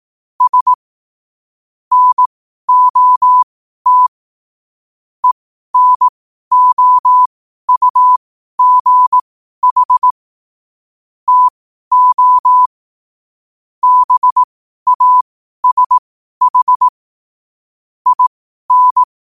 Morse code, heard as a keyed tone, about 18 wpm, spaced as for 12 wpm, 1000 Hz. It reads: S NOT ENOUGH TO BASH IN
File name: qotdfiles / 12wpm